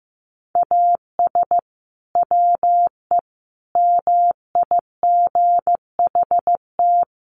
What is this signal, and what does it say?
Morse code, heard as a keyed tone, about 15 wpm, 700 Hz. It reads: AS WE MIGHT
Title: Sentences from Top 1100 Words 15wpm